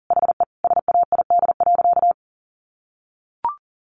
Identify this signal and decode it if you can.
Morse code, heard as a keyed tone, about 40 wpm, 700 Hz. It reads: HE SAID.
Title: Sentences from Top 600 Words 40wpm